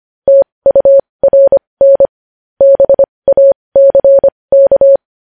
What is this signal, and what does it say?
Morse code, heard as a keyed tone, about 25 wpm, spaced as for 20 wpm, 550 Hz. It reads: TURN BACK
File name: podcast / News headlines-1x-25x20